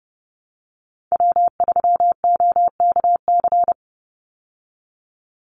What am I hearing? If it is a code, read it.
Morse code, heard as a keyed tone, about 30 wpm, 700 Hz. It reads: W3OKC
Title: Call Signs Repeated 3x - 30wpm